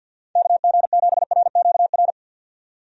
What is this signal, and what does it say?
Morse code, heard as a keyed tone, about 53 wpm, 700 Hz. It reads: KG7RQR
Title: USA Call Signs - 6 Characters - 50wpm